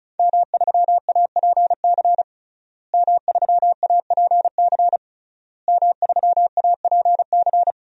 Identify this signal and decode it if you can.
Morse code, heard as a keyed tone, about 36 wpm, 700 Hz. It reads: M3APC M3APC M3APC
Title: DXCC-100 Call Signs - 5 Characters - Repeated 3x - 35wpm